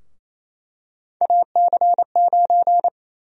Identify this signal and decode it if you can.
Morse code, heard as a keyed tone, about 28 wpm, 700 Hz. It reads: AC9
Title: USA Call Sign Prefixes - 28wpm